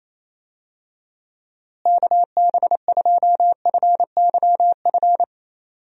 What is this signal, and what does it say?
Morse code, heard as a keyed tone, about 28 wpm, 700 Hz. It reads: KB2FYF
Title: USA Call Signs - 6 Characters - Repeated 3x - 28wpm